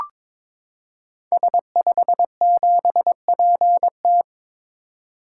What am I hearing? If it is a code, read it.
Morse code, heard as a keyed tone, about 22 wpm, 700 Hz. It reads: S57PT
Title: Code Groups - 5 Letter:Numbers - 22wpm